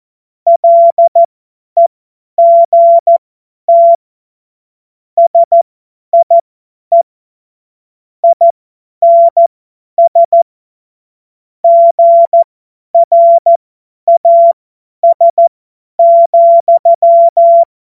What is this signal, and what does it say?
Morse code, heard as a keyed tone, about 14 wpm, spaced as for 10 wpm, 680 Hz. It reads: LEGT SIE INS GRAS,
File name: Morsekurs / Witze / Witz 21. Juni 2022 50 F70